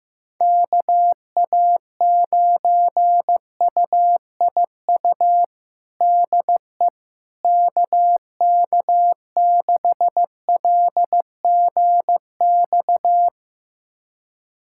Morse code, heard as a keyed tone, about 15 wpm, 700 Hz. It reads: KA9UIU DE KK6LGX
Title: USA Callsign DE Callsign - 6 Characters - 15wpm